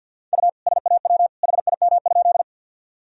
Morse code, heard as a keyed tone, about 52 wpm, 700 Hz. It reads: U SAW HIM?
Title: Sentences from Top 300 Words and Abbreviations 50wpm